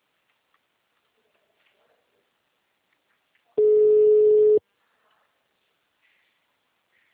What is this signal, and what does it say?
Ringback tone (Europe, 425 Hz).